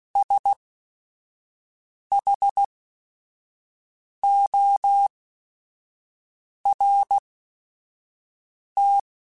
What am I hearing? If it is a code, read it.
Morse code, heard as a keyed tone, about 16 wpm, spaced as for 5 wpm, 800 Hz. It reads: SHORT